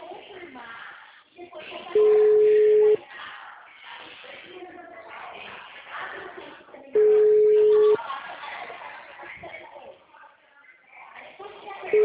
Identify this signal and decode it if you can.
Ringback tone (Europe, 425 Hz), with speech.
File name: out- -2011- - - .280